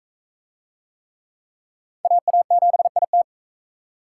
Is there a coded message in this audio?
Yes: Morse code from 2.0 to 3.2 s, heard as a keyed tone, about 40 wpm, 700 Hz. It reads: AA7IT